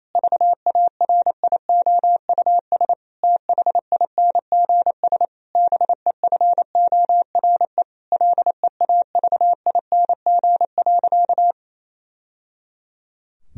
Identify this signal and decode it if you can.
Morse code, heard as a keyed tone, about 28 wpm, 700 Hz. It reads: VARIOUS THINGS BEFORE LEAVING.
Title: Sentences from Top 1000 Words - Review Part 4 - 28wpm